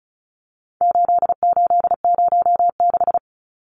Morse code, heard as a keyed tone, about 34 wpm, 700 Hz. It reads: 8806
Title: Numbers - 4-Digit 35wpm